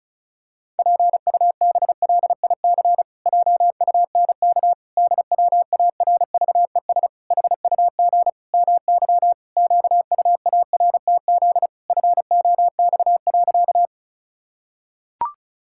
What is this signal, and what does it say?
Morse code, heard as a keyed tone, about 34 wpm, 700 Hz. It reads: PUBLIC JUNK DWARVES HUG MY QUARTZ FOX.
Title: Pangram Challenge v2 - 35wpm